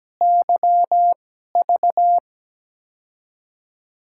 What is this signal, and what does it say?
Morse code, heard as a keyed tone, about 17 wpm, spaced as for 12 wpm, 700 Hz. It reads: YV
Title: USA Call Signs - 5 Characters - 2x Character Spacing - 17wpm